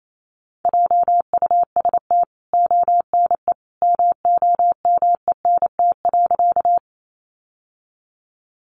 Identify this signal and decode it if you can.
Morse code, heard as a keyed tone, about 28 wpm, 700 Hz. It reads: JUST ONE MOMENT.
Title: Sentences from Top 1000 Words 28wpm